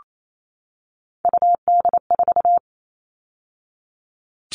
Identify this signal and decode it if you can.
Morse code, heard as a keyed tone, about 28 wpm, 700 Hz. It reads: UD4
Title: All Cut Numbers - 3-Digit 28wpm